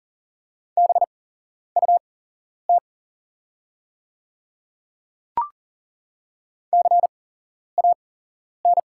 Morse code, heard as a keyed tone, about 40 wpm, spaced as for 11 wpm, 690 Hz. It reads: BUT CAN
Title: 8x Character Spacing - Top 100 Words - 40wpm